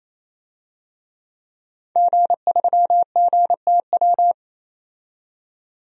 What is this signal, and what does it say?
Morse code, heard as a keyed tone, about 28 wpm, 700 Hz. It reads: G3GTW
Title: DXCC-100 Call Signs - 5 Characters - 28wpm